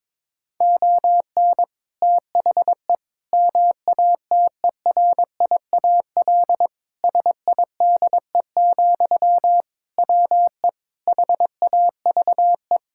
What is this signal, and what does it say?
Morse code, heard as a keyed tone, about 22 wpm, 700 Hz. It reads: ON THE MATERIAL SIDE, WE HAVE
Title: Sentences from Top 1000 Words - Review Part 5 - 22wpm